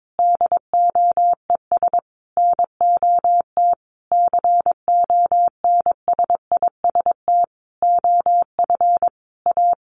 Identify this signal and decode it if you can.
Morse code, heard as a keyed tone, about 22 wpm, 700 Hz. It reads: DOES NOT CONSIST OF A
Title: Sentences from Top 1400 Words 22wpm